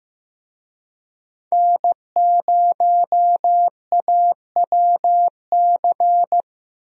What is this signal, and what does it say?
Morse code, heard as a keyed tone, about 14 wpm, 700 Hz. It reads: N0AWC
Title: USA Call Signs - 5 Characters - Repeated 2x - 15wpm